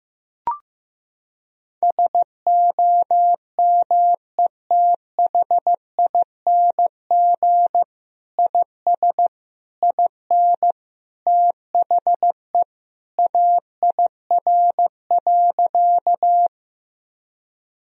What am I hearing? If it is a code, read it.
Morse code, heard as a keyed tone, about 16 wpm, 700 Hz. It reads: SOMETHING IS IN THE AIR.
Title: Sentences from Top 1000 Words - Review Part 1 - 15wpm